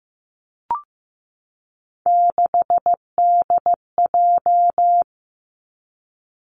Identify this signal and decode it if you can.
Morse code, heard as a keyed tone, about 16 wpm, 700 Hz. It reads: 6DJ
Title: Three Challenging Letter-Number Combinations 15wpm